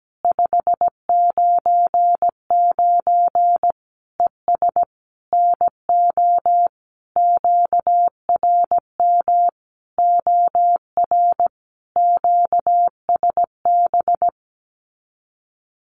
Morse code, heard as a keyed tone, about 17 wpm, 700 Hz. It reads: 599 ES NO QRM OR QSB
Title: Common QSO Phrases 17wpm